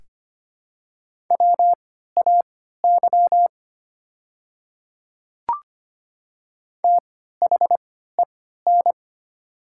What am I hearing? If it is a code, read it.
Morse code, heard as a keyed tone, about 25 wpm, spaced as for 14 wpm, 700 Hz. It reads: WAY THEN
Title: 3x Character Spacing - Top 500 Words - 25wpm